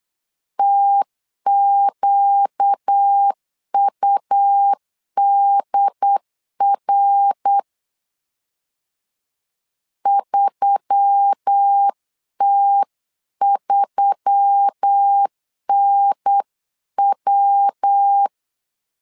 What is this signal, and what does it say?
Morse code, heard as a keyed tone, about 8 wpm, 800 Hz. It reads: TQUDR 3T3NW
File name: LEZIONI / 4 Q1S3 8wpm 3037